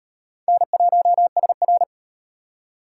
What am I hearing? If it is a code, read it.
Morse code, heard as a keyed tone, about 36 wpm, 700 Hz. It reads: N1SR